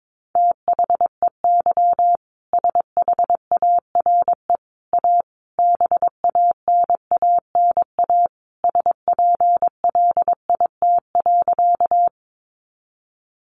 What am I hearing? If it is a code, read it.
Morse code, heard as a keyed tone, about 22 wpm, 700 Hz. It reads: THEY SHARE A BANANA SPLIT.
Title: Easy Phrases 22wpm